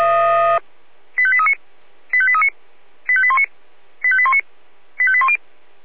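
Five-tone selective call (ZVEI), heard as quick runs of five tones, tones of about 70 ms. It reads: 85839 85839 85819 85819 85810
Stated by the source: ZVEI five-tone selective call received on SDR.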